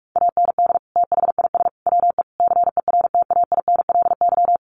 Morse code, heard as a keyed tone, about 44 wpm, 700 Hz. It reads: AND THIS WE CERTAINLY
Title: Sentences from Top 900 Words 45wpm